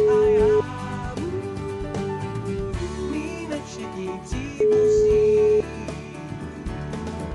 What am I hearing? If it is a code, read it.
Ringback tone (Europe, 425 Hz).